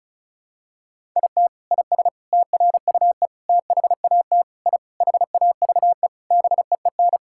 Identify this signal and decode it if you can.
Morse code, heard as a keyed tone, about 34 wpm, 700 Hz. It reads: IT IS TRUE THAT I HAVE BEEN
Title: Sentences from Top 1000 Words - Review Part 6 - 35wpm